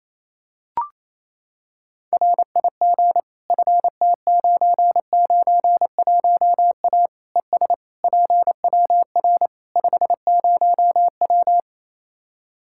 Morse code, heard as a keyed tone, about 28 wpm, 700 Hz. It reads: RIG FT991A ES PWR 50W
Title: K1USN SST QSO Practice 28wpm